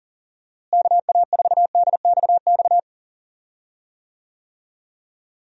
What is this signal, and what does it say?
Morse code, heard as a keyed tone, about 40 wpm, 700 Hz. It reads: KA4DXX
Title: USA Call Signs - 6 Characters - Repeated 3x - 40wpm